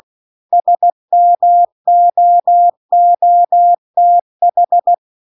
Morse code, heard as a keyed tone, about 16 wpm, 700 Hz. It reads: SMOOTH